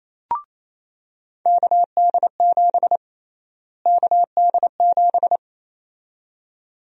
Morse code, heard as a keyed tone, about 28 wpm, 700 Hz. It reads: KD7 KD7
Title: USA Call Sign Prefixes - Repeated 2x - 28wpm